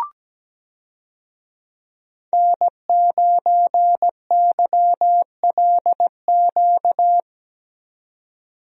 Morse code, heard as a keyed tone, about 17 wpm, 700 Hz. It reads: N9YLQ
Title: USA Call Signs - 5 Characters - 17wpm